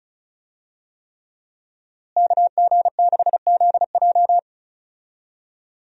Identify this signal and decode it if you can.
Morse code, heard as a keyed tone, about 36 wpm, 700 Hz. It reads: KG6ZJ
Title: USA Call Signs - 5 Characters - Repeated 3x - 35wpm